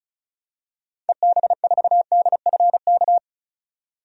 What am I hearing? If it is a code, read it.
Morse code, heard as a keyed tone, about 34 wpm, 700 Hz. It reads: EB4DFK
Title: DXCC-100 Call Signs - 6 Characters - 35wpm